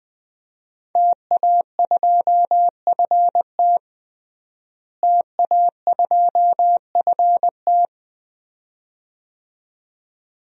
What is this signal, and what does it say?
Morse code, heard as a keyed tone, about 20 wpm, 700 Hz. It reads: TA2FT TA2FT
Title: DXCC-100 Call Signs - 5 Characters - Repeated 2x - 20wpm